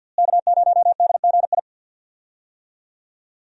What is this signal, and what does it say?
Morse code, heard as a keyed tone, about 48 wpm, 700 Hz. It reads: K0DGI
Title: USA Call Signs - 5 Characters - Repeated 2x - 50wpm